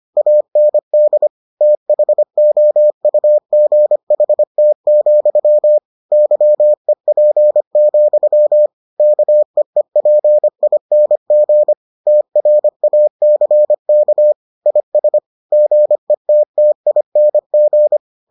Morse code, heard as a keyed tone, about 25 wpm, 600 Hz. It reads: AND THOUGHT, YEP, KEEPING TRACK IS GETTING